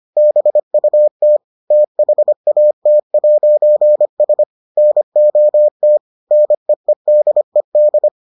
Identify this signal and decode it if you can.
Morse code, heard as a keyed tone, about 25 wpm, 600 Hz. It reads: BUT THAT'S NOT NEEDED